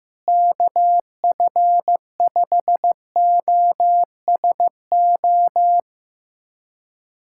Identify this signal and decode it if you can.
Morse code, heard as a keyed tone, about 15 wpm, 700 Hz. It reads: KF5OSO
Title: USA Callsign DE Callsign 15wpm